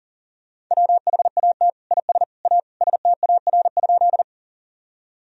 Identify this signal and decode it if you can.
Morse code, heard as a keyed tone, about 40 wpm, 700 Hz. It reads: WHAT IS A STAR?
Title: Sentences from Top 1000 Words - Review Part 2 - 40wpm